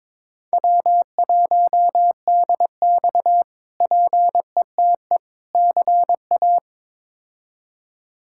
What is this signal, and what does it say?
Morse code, heard as a keyed tone, about 22 wpm, 700 Hz. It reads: W1DX PETE CA
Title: K1USN SST QSO Practice - 22wpm